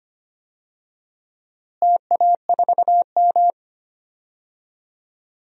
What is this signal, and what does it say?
Morse code, heard as a keyed tone, about 25 wpm, 700 Hz. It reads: TA4M
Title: DXCC-100 Call Signs - 4 Characters - Repeated 3x - 25wpm